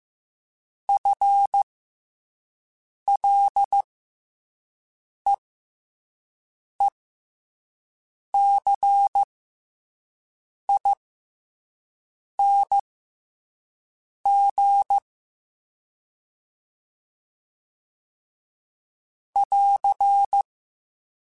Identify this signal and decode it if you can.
Morse code, heard as a keyed tone, about 15 wpm, spaced as for 5 wpm, 800 Hz. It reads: FLEECING <AR>
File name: FL / FL mp3s / FL Prefixes 1x1 @5